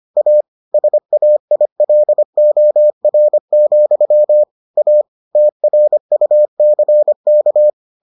Morse code, heard as a keyed tone, about 25 wpm, 600 Hz. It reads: A SAILOR, A TRUCK